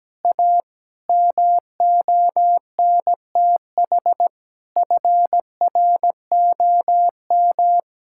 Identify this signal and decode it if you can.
Morse code, heard as a keyed tone, about 17 wpm, 700 Hz. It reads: A MONTH FROM
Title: Sentences from Top 1000 Words - Review Part 5 - 17wpm